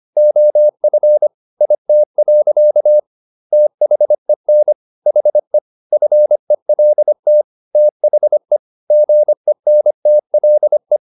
Morse code, heard as a keyed tone, about 25 wpm, 600 Hz. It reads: OF IT. THEN HE FELT THE GENTLE
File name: the old man and the sea part 1 25wpm